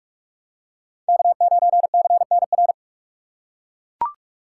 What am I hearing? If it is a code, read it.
Morse code, heard as a keyed tone, about 41 wpm, 700 Hz. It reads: K9CNR
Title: USA Call Signs - 5 Characters - 3x Character Spacing - 45wpm